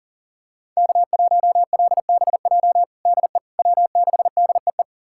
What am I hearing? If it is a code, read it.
Morse code, heard as a keyed tone, about 40 wpm, 700 Hz. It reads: K1LBJ DE W6DEE
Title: USA Callsign DE Callsign - 5 Characters - 40wpm